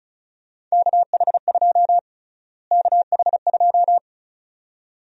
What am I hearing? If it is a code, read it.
Morse code, heard as a keyed tone, about 34 wpm, 700 Hz. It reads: KH2 KH2
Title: USA Call Sign Prefixes - Repeated 2x - 35wpm